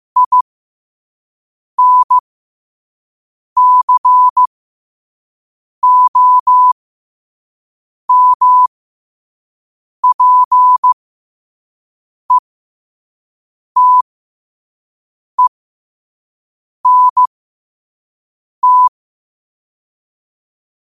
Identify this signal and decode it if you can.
Morse code, heard as a keyed tone, about 15 wpm, spaced as for 5 wpm, 1000 Hz. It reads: INCOMPETENT
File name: qotdfiles / 5wpm